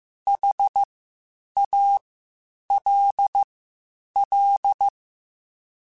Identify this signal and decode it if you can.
Morse code, heard as a keyed tone, about 15 wpm, spaced as for 8 wpm, 800 Hz. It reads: HALL
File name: H Prefixes / H mp3s